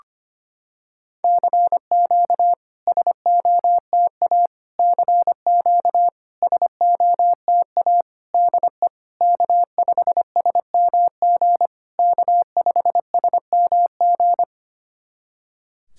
Morse code, heard as a keyed tone, about 25 wpm, 690 Hz. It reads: CQ SOTA CQ SOTA DE K5SMG K5SMG
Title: SOTA QSO Practice 25wpm